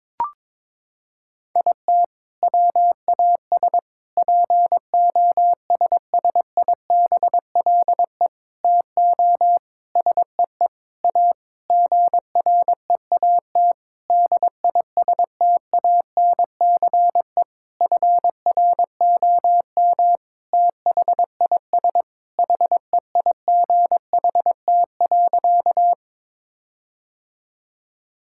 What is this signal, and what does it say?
Morse code, heard as a keyed tone, about 21 wpm, 700 Hz. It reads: IT WAS POSSIBLE TO SEE A GREAT DISTANCE FROM THIS HEIGHT.
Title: Sentences from Top 1600 Words 22wpm